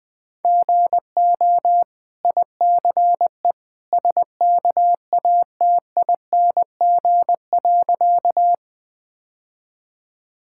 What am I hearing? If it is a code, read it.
Morse code, heard as a keyed tone, about 20 wpm, 700 Hz. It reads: GO ICE SKATING.
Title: Easy Sentences with Hints - 20wpm